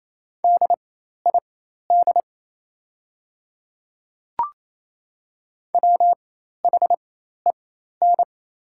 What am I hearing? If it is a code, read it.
Morse code, heard as a keyed tone, about 28 wpm, spaced as for 13 wpm, 700 Hz. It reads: DID WHEN
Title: 4x Character Spacing - Top 100 Words - 28wpm